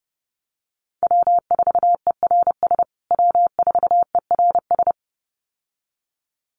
Morse code, heard as a keyed tone, about 30 wpm, 700 Hz. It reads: W4ERS W4ERS
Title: USA Call Signs Repeated 2x 30wpm